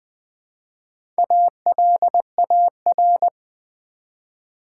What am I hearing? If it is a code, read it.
Morse code, heard as a keyed tone, about 20 wpm, 690 Hz. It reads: ALAR
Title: Build A Word - 1 Letter At A Time - 5-Letter Words - 20wpm